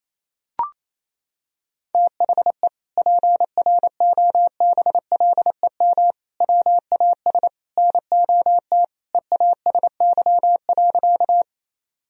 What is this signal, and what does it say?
Morse code, heard as a keyed tone, about 28 wpm, 700 Hz. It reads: THE PROBLEM WAS NOT EASY.
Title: Sentences from Top 1000 Words - Review Part 2 - 28wpm